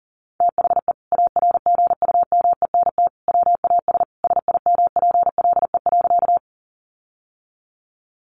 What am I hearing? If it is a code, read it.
Morse code, heard as a keyed tone, about 37 wpm, 700 Hz. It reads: THE ARGUMENT WAS SIMPLE.
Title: Sentences from Top 1400 Words 40wpm